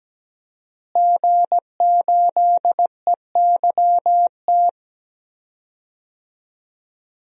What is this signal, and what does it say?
Morse code, heard as a keyed tone, about 17 wpm, 700 Hz. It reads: G8EYT
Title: DXCC-100 Call Signs - 5 Characters - Repeated 3x - 17wpm